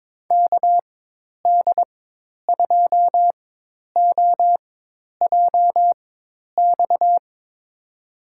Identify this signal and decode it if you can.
Morse code, heard as a keyed tone, about 22 wpm, spaced as for 10 wpm, 700 Hz. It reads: KD2OJX
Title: USA Call Signs - 4x Character Spacing - 22wpm